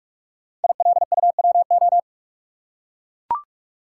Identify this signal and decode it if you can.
Morse code, heard as a keyed tone, about 44 wpm, 700 Hz. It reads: ILUWO
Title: Code Groups - 5 Letters - 45wpm